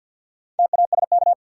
Morse code, heard as a keyed tone, about 53 wpm, 700 Hz. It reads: TASK
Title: Sentences from Top 1000 Words - Review Part 7 - New Word Drill - 50wpm